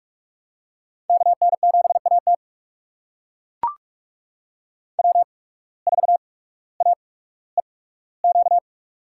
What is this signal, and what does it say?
Morse code, heard as a keyed tone, about 44 wpm, spaced as for 19 wpm, 700 Hz. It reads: KN7AT W4AEQ
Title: USA Call Signs - 5 Characters - 8x Character Spacing - 45wpm